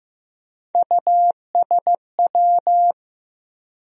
Morse code, heard as a keyed tone, about 15 wpm, 700 Hz. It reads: USW
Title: USA Call Sign Postfixes - 3 Letters 15wpm